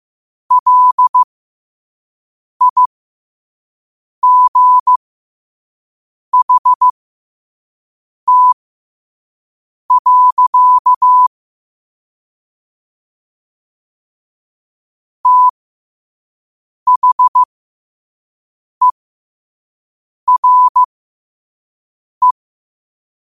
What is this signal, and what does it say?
Morse code, heard as a keyed tone, about 15 wpm, spaced as for 5 wpm, 1000 Hz. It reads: LIGHT. THERE
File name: qotdfiles / 5wpm